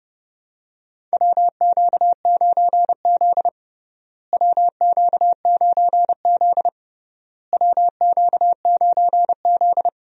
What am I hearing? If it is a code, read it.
Morse code, heard as a keyed tone, about 30 wpm, 700 Hz. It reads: WQ9Z WQ9Z WQ9Z
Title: USA Call Signs - 4 Characters - Repeated 3x - 30wpm